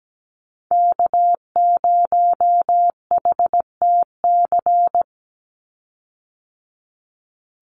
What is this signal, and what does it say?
Morse code, heard as a keyed tone, about 17 wpm, 700 Hz. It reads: K0HTC
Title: USA Call Signs 17wpm